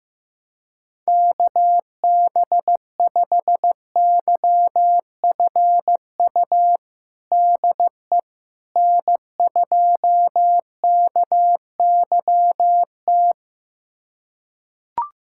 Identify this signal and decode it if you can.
Morse code, heard as a keyed tone, about 15 wpm, 700 Hz. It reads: KB5YFU DE N2KYT
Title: USA Callsign DE Callsign 15wpm